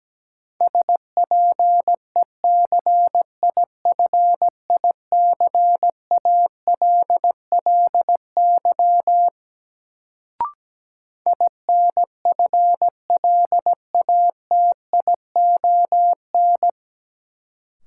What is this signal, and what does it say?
Morse code, heard as a keyed tone, about 17 wpm, 700 Hz. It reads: SPECIFICALLY INFLATION
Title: Sentences from Top 1100 Words - New Word Drill - 17wpm